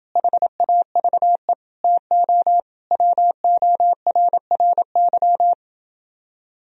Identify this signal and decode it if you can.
Morse code, heard as a keyed tone, about 28 wpm, 700 Hz. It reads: HAVE TO WORRY